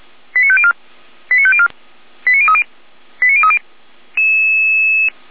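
Five-tone selective call (ZVEI), heard as quick runs of five tones, tones of about 70 ms. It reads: 89504 89504 89030 89030
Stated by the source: ZVEI five-tone selective call received by SDR at fire brigade Orlamünde (feuerwehr-orlamuende).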